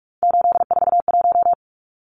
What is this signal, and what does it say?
Morse code, heard as a keyed tone, about 44 wpm, 700 Hz. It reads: 841
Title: Numbers - 3-Digit 45wpm